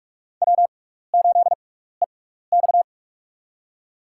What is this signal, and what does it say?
Morse code, heard as a keyed tone, about 44 wpm, spaced as for 15 wpm, 700 Hz. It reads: W8EX
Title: USA Call Signs - 6x Character Spacing - 45wpm